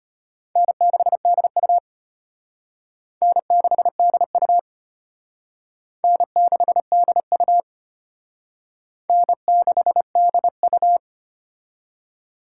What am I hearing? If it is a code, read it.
Morse code, heard as a keyed tone, about 32 wpm, 700 Hz. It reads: N6DU N6DU N6DU N6DU